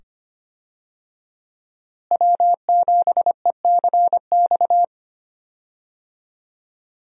Morse code, heard as a keyed tone, about 25 wpm, 700 Hz. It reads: W7ECX
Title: USA Call Signs - 5 Characters - Repeated 2x - 25wpm